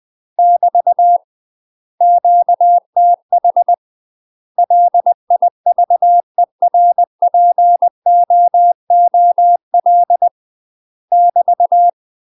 Morse code, heard as a keyed tone, about 20 wpm, 700 Hz. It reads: <BT> QTH LIVERPOOL <BT>
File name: UK MASTER QSOs TRAINING / 20WPM / UK MASTER QSOs 0065